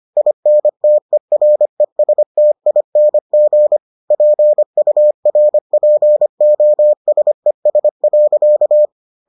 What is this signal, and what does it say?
Morse code, heard as a keyed tone, about 25 wpm, 600 Hz. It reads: INTERESTING PURPOSES.